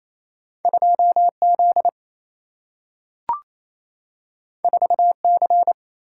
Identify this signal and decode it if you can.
Morse code, heard as a keyed tone, about 28 wpm, 700 Hz. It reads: 2Z 4C
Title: Single-Letter Single-Number 28wpm